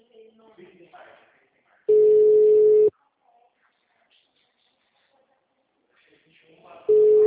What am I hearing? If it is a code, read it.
Ringback tone (Europe, 425 Hz), with speech.